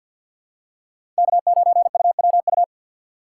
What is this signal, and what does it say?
Morse code, heard as a keyed tone, about 52 wpm, 700 Hz. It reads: K9UWU
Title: USA Call Signs - 5 Characters - 3x Character Spacing - 50wpm